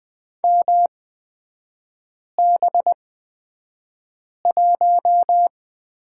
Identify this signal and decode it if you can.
Morse code, heard as a keyed tone, about 20 wpm, spaced as for 5 wpm, 690 Hz. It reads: MB1
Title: 229 Lesson 37e - Letter x - Three Characters 20wpm